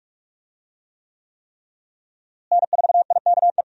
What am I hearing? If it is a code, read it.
Morse code, heard as a keyed tone, about 46 wpm, 700 Hz. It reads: N4IKE
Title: USA Call Signs - 5 Characters - Repeated 3x - 45wpm